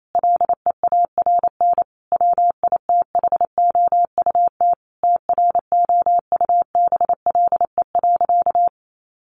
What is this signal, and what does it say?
Morse code, heard as a keyed tone, about 28 wpm, 700 Hz. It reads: LEARN WITHOUT TROUBLE.
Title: Sentences from Top 1200 Words 28wpm